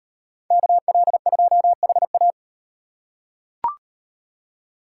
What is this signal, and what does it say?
Morse code, heard as a keyed tone, about 38 wpm, 700 Hz. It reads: KL2HA